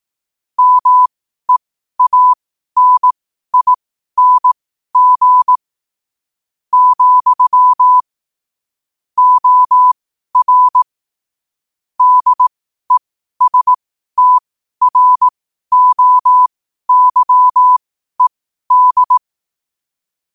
Morse code, heard as a keyed tone, about 18 wpm, spaced as for 12 wpm, 1000 Hz. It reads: MEANING, OR DESTROYED
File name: qotdfiles / 12wpm